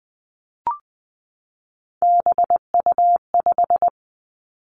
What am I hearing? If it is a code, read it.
Morse code, heard as a keyed tone, about 20 wpm, 700 Hz. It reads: BU5
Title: Three Challenging Letter-Number Combinations 20wpm